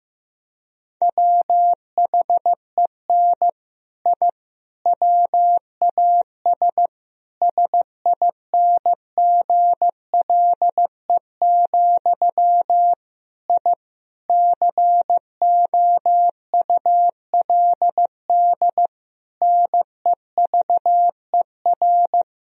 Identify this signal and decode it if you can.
Morse code, heard as a keyed tone, about 16 wpm, 700 Hz. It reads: WHEN I WAS SINGLE, I COULD NEVER
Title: Sentences from Top 1000 Words - Review Part 4 - 15wpm